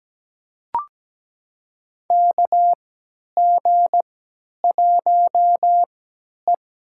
Morse code, heard as a keyed tone, about 17 wpm, spaced as for 10 wpm, 700 Hz. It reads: KG1E
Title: USA Call Signs - 3x Character Spacing - 17wpm